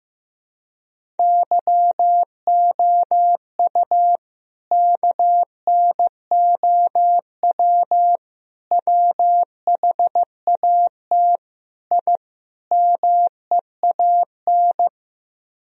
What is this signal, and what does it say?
Morse code, heard as a keyed tone, about 15 wpm, 700 Hz. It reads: YOU KNOW WHAT I MEAN
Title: Speed-Racing - Sets of 5 Words 15wpm